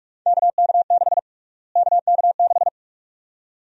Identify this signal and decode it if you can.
Morse code, heard as a keyed tone, about 46 wpm, 700 Hz. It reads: KK6 KK6
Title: USA Call Sign Prefixes - Repeated 2x - 45wpm